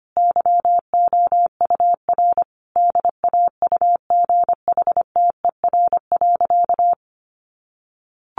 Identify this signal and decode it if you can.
Morse code, heard as a keyed tone, about 25 wpm, 700 Hz. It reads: YOUR DAUGHTER.
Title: Sentences from Top 1500 Words 25wpm